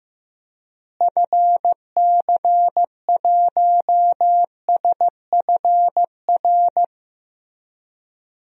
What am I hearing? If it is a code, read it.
Morse code, heard as a keyed tone, about 15 wpm, 700 Hz. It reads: FC1SFR
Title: DXCC-100 Call Signs - 6 Characters - Repeated 3x - 15wpm